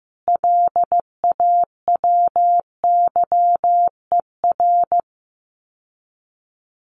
Morse code, heard as a keyed tone, about 15 wpm, 700 Hz. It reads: LAWYER
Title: Common QSO Phrases Encore 15wpm